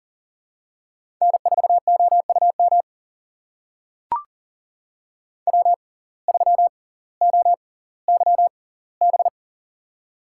Morse code, heard as a keyed tone, about 40 wpm, spaced as for 20 wpm, 700 Hz. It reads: N4OUM W3OYB
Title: USA Call Signs - 5 Characters - 6x Character Spacing - 40wpm